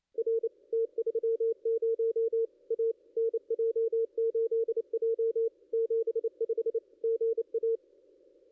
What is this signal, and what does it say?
Morse code, heard as a keyed tone, about 28 wpm, 440 Hz. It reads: R T30 A NJ8J 75 GA
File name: contests / ss-cw-k5tr-op-n5rz -2200